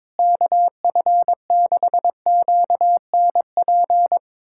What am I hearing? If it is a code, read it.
Morse code, heard as a keyed tone, about 22 wpm, 700 Hz. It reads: KF6QNP
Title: USA Callsign DE Callsign - 6 Characters - 22wpm